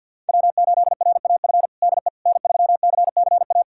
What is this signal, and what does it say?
Morse code, heard as a keyed tone, about 47 wpm, 700 Hz. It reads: W8RAF DE N3CCA
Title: USA Callsign DE Callsign - 5 Characters - 50wpm